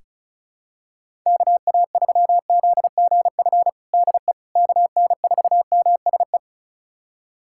Morse code, heard as a keyed tone, about 36 wpm, 700 Hz. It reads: KA3ZGF DE KN4MSE
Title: USA Callsign DE Callsign 35wpm